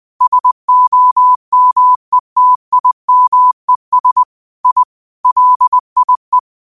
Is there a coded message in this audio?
Yes: Morse code from 0.2 to 6.4 s, heard as a keyed tone, about 20 wpm, 1000 Hz. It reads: SOMETIMES I LIE